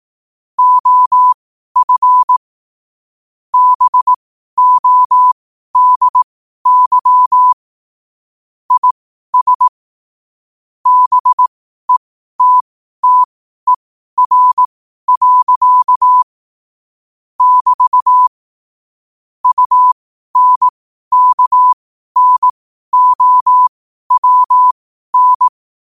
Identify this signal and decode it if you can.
Morse code, heard as a keyed tone, about 18 wpm, spaced as for 12 wpm, 1000 Hz. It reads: OF BODY IS BETTER. <BT> UNKNOWN
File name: qotdfiles / 12wpm